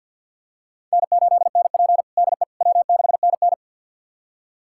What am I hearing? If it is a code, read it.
Morse code, heard as a keyed tone, about 46 wpm, 700 Hz. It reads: N8NP DE W6NN
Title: USA Callsign DE Callsign - 4 Characters - 50wpm